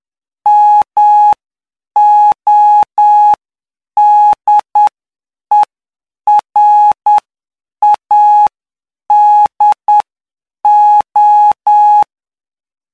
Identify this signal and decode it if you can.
Morse code, heard as a keyed tone, about 9 wpm, spaced as for 7 wpm, 840 Hz. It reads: MODERADO
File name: cw / py2lcd / 08-PY2LCD-Li��o 17